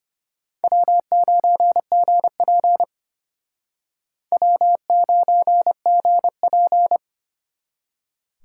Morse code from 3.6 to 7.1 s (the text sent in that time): W9GP